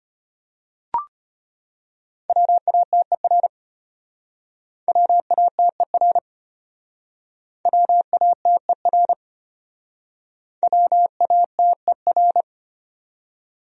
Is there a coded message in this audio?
Yes: Morse code from 2.3 to 12.4 s, heard as a keyed tone, about 32 wpm, 700 Hz. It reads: WATER WATER WATER WATER